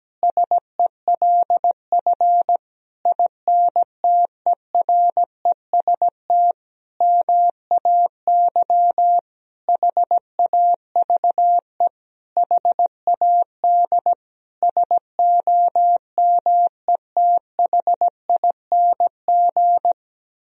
Morse code, heard as a keyed tone, about 17 wpm, 700 Hz. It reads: SELF INTEREST MAY HAVE HAD SOMETHING
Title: Sentences from Top 1000 Words - Review Part 2 - 17wpm